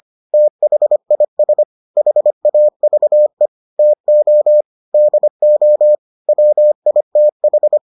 Morse code, heard as a keyed tone, about 25 wpm, 600 Hz. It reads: THIS HAVE TO DO WITH